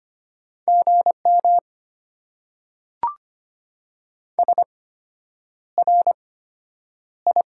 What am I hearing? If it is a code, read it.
Morse code, heard as a keyed tone, about 25 wpm, spaced as for 9 wpm, 700 Hz. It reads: GM SRI
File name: Top 90 QSO Elements - 8x Character Spacing - 25wpm